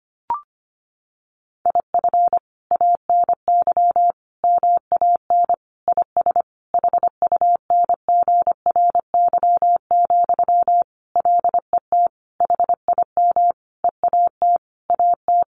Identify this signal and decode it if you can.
Morse code, heard as a keyed tone, about 25 wpm, 700 Hz. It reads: IF ANY MAN IS HUNGRY, LET HIM EAT AT
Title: Sentences from Top 1400 Words 25wpm